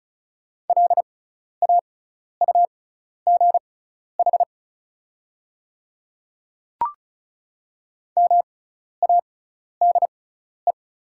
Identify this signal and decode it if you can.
Morse code, heard as a keyed tone, about 34 wpm, spaced as for 12 wpm, 690 Hz. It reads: LAUGH MADE
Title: 6x Character Spacing - Top 500 Words - 35wpm